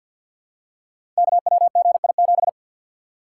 Morse code, heard as a keyed tone, about 51 wpm, 700 Hz. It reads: KWGI7
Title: Code Groups - 5 Letter:Numbers - 50wpm